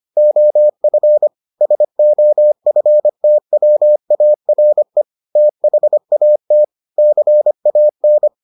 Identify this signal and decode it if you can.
Morse code, heard as a keyed tone, about 25 wpm, 600 Hz. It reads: OF SOFTWARE THAT CAN